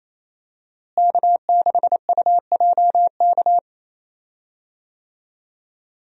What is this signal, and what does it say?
Morse code, heard as a keyed tone, about 28 wpm, 700 Hz. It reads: K6UJK
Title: USA Call Signs - 5 Characters - Repeated 2x - 28wpm